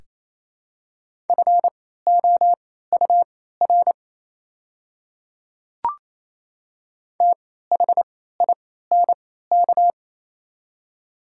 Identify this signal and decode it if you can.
Morse code, heard as a keyed tone, about 27 wpm, spaced as for 16 wpm, 700 Hz. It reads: FOUR THINK
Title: 3x Character Spacing - Top 300 Words - 28wpm